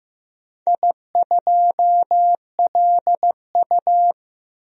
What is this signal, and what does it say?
Morse code, heard as a keyed tone, about 15 wpm, 700 Hz. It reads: I2LU
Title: DXCC-100 Call Signs - 4 Characters - Repeated 2x - 15wpm